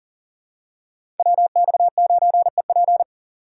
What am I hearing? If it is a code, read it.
Morse code, heard as a keyed tone, about 37 wpm, 700 Hz. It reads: WX9EP
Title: USA Call Signs - 5 Characters - 6x Character Spacing - 40wpm